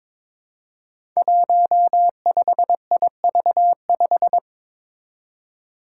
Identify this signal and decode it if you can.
Morse code, heard as a keyed tone, about 21 wpm, 700 Hz. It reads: 15IV5
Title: Five Challenging Letter-Number Combinations 22wpm